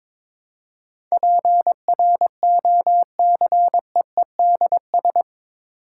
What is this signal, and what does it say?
Morse code, heard as a keyed tone, about 22 wpm, 700 Hz. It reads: PROCEEDS
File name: practice / 22wpm / Eight Letter Words - 22wpm